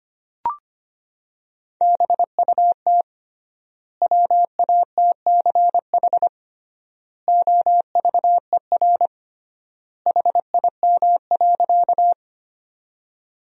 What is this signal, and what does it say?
Morse code, heard as a keyed tone, about 25 wpm, 700 Hz. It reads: BUT WATCH OVER HIM.
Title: 3x Word-Spacing - Sentences from Top 300 Words - 25wpm